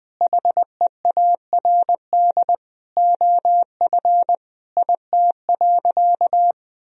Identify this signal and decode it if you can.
Morse code, heard as a keyed tone, about 20 wpm, 700 Hz. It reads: HEARD OF IT.
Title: Sentences from Top 1000 Words - Review Part 3 - 20wpm